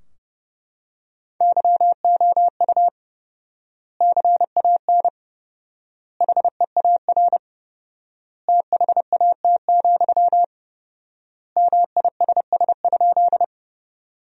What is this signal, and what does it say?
Morse code, heard as a keyed tone, about 31 wpm, 700 Hz. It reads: YOU CAN HEAR THAT, MISS?
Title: 4x Word-Spacing - Sentences from Top 500 Words - 30wpm